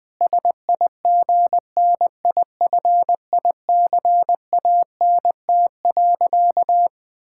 Morse code, heard as a keyed tone, about 20 wpm, 700 Hz. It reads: SIGNIFICANT.
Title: Sentences from Top 1000 Words - Review Part 6 - 20wpm